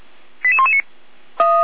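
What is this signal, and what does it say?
Five-tone selective call (ZVEI), heard as quick runs of five tones, tones of about 80 ms. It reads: 88119
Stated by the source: ZVEI five-tone selective call 88112, received on an SDR.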